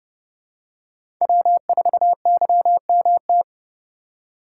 Morse code, heard as a keyed tone, about 30 wpm, 700 Hz. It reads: W4YMT
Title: USA Call Signs - 5 Characters - 2x Character Spacing - 30wpm